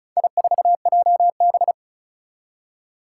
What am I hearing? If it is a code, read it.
Morse code, heard as a keyed tone, about 36 wpm, 700 Hz. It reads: I4JB
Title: Four Challenging Letter-Number Combinations 35wpm